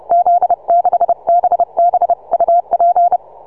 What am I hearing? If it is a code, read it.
Morse code, heard as a keyed tone, about 31 wpm, 690 Hz. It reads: Z6BB UP
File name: soundclips / TZ6BB 6MCW